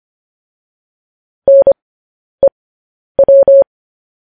Morse code, heard as a keyed tone, about 25 wpm, spaced as for 10 wpm, 550 Hz. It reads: NEW
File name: podcast / News headlines-1x-25x10